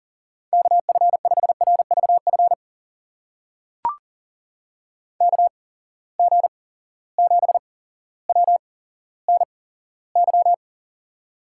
Morse code, heard as a keyed tone, about 40 wpm, spaced as for 17 wpm, 700 Hz. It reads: KF5RVF KG7WNY
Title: USA Call Signs - 6 Characters - 8x Character Spacing - 40wpm